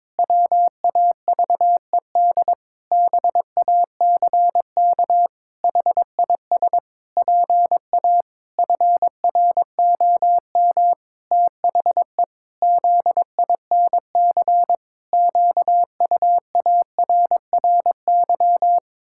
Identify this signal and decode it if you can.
Morse code, heard as a keyed tone, about 22 wpm, 700 Hz. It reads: WAVED BACK HIS PA FROM THE ZINC QUARRY
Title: Pangram Challenge v2 - 22wpm